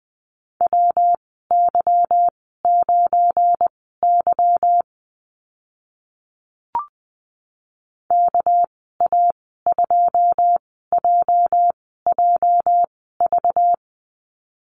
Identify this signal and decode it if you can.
Morse code, heard as a keyed tone, about 20 wpm, spaced as for 14 wpm, 700 Hz. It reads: WY9Y KA2JJV
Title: 207 Lesson 33g - Letter j - Call Signs 20wpm